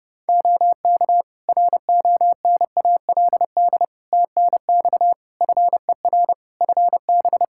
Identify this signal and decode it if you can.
Morse code, heard as a keyed tone, about 32 wpm, 700 Hz. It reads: OK RONALD TNX FER FB
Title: Practice QSO - Exchange 4 - Ending - 30wpm